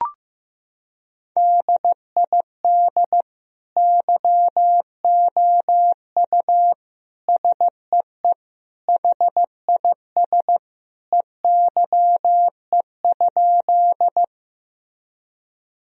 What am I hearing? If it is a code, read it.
Morse code, heard as a keyed tone, about 15 wpm, 700 Hz. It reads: DID YOU SEE HIS EYE?
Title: Sentences from Top 1000 Words - Review Part 1 - 15wpm